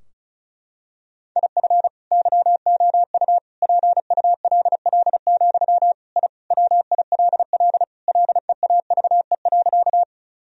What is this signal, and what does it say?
Morse code, heard as a keyed tone, about 34 wpm, 700 Hz. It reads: IF YOU PULL, I WILL LEAVE.
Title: Sentences from Top 1000 Words - Review Part 2 - 35wpm